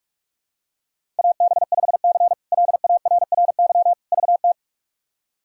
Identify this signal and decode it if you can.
Morse code, heard as a keyed tone, about 42 wpm, 700 Hz. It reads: AB5C LARRY VT
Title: K1USN SST QSO Practice - 45wpm